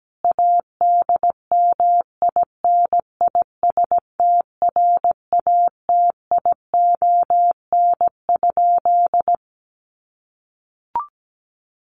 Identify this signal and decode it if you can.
Morse code, heard as a keyed tone, about 17 wpm, 700 Hz. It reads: ADMINISTRATION?
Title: Sentences from Top 1300 Words 17wpm